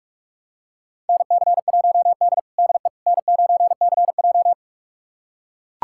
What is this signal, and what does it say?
Morse code, heard as a keyed tone, about 44 wpm, 690 Hz. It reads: NC1D DE N9CJ
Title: USA Callsign DE Callsign - 4 Characters - 45wpm